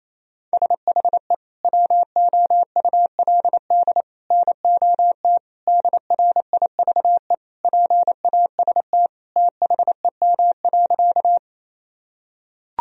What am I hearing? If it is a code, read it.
Morse code, heard as a keyed tone, about 28 wpm, 700 Hz. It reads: SHE WOULD NOT DRIVE PAST THEM.
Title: Sentences from Top 1000 Words - Review Part 3 - 28wpm